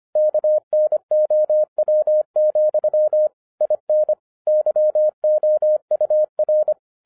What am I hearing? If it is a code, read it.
Morse code, heard as a keyed tone, about 25 wpm, 610 Hz. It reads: KNOW, IN YOUR
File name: 2012-01-mp3 / n14-FUN WITH NUMBERS-25wpm